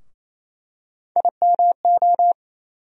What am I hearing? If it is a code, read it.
Morse code, heard as a keyed tone, about 28 wpm, 700 Hz. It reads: IMO
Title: USA Call Sign Postfixes - 3 Letters 28wpm